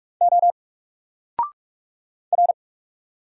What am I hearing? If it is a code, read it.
Morse code, heard as a keyed tone, about 44 wpm, 700 Hz. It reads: O R
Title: Single Character (Letter:Number:Punctuation:Prosign) 45wpm